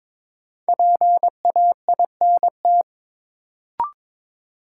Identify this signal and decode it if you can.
Morse code, heard as a keyed tone, about 22 wpm, 700 Hz. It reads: PAINT
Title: Sentences from Top 1000 Words - Review Part 6 - New Word Drill - 22wpm